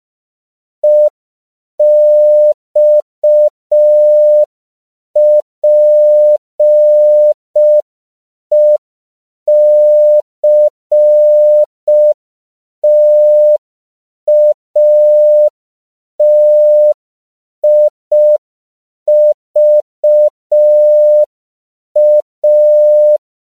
Morse code, heard as a keyed tone, about 5 wpm, 600 Hz. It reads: EXPECTATIVA
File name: 5PPM